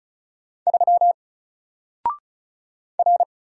Morse code, heard as a keyed tone, about 36 wpm, 700 Hz. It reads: 3 R
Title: Single Character (Letter:Number:Punctuation:Prosign) 35wpm